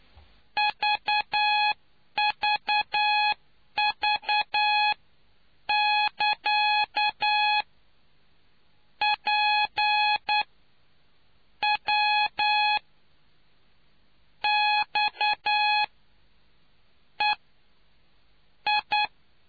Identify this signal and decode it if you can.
Morse code, heard as a keyed tone, about 9 wpm, spaced as for 6 wpm, 840 Hz. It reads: VVV <KA> PWXEI